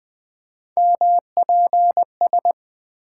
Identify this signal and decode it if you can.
Morse code, heard as a keyed tone, about 20 wpm, 700 Hz. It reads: MPS
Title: Three Letter-Number Combinations - 20wpm